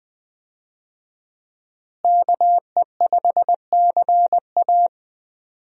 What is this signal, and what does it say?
Morse code, heard as a keyed tone, about 20 wpm, 700 Hz. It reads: KE5CA
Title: USA Call Signs - 5 Characters - Repeated 2x - 20wpm